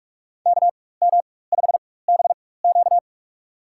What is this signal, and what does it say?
Morse code, heard as a keyed tone, about 46 wpm, spaced as for 21 wpm, 700 Hz. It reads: KM5BQ
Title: USA Call Signs - 5 Characters - 4x Character Spacing - 45wpm